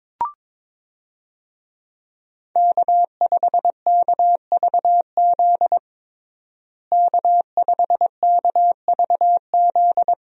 Morse code, heard as a keyed tone, about 21 wpm, 700 Hz. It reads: K5KVZ K5KVZ
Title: USA Call Signs - 5 Characters - Repeated 3x - 22wpm